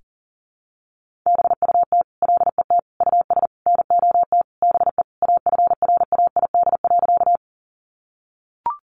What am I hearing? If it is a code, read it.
Morse code, heard as a keyed tone, about 40 wpm, 700 Hz. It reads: BUT LET US NOT BE AFRAID.
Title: Sentences from Top 1500 Words 40wpm